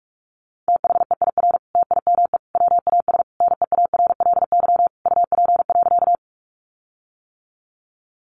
Morse code, heard as a keyed tone, about 44 wpm, 700 Hz. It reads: THEIR TIME WAS NEARLY UP.
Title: Sentences from Top 1300 Words 45wpm